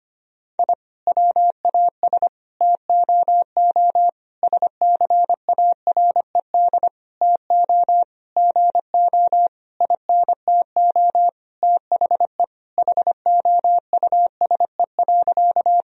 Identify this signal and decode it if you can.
Morse code, heard as a keyed tone, about 25 wpm, 700 Hz. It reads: I WAS TOO SCARED TO GO INTO THE HOUSE.
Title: Sentences from Top 1000 Words - Review Part 4 - 25wpm